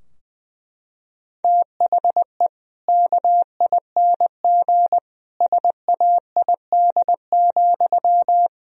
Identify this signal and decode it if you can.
Morse code, heard as a keyed tone, about 20 wpm, 700 Hz. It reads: THE KING SAID,
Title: Sentences from Top 1000 Words - Review Part 3 - 20wpm